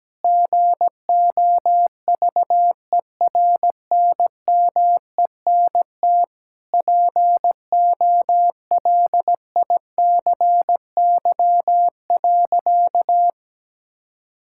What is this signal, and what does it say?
Morse code, heard as a keyed tone, about 17 wpm, 700 Hz. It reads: GOVERNMENT POLICY.
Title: Sentences from Top 1000 Words - Review Part 5 - 17wpm